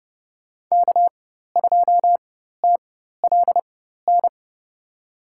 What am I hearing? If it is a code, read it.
Morse code, heard as a keyed tone, about 32 wpm, spaced as for 14 wpm, 700 Hz. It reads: K2TLN
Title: USA Call Signs - 5 Characters - 4x Character Spacing - 30wpm